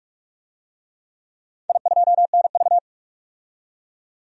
Morse code, heard as a keyed tone, about 41 wpm, 700 Hz. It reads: I2NV
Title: DXCC-100 Call Signs - 4 Characters - 45wpm